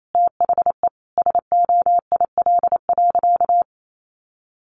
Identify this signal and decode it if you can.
Morse code, heard as a keyed tone, about 28 wpm, 700 Hz. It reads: THE SOIL.
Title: Sentences from Top 1500 Words 28wpm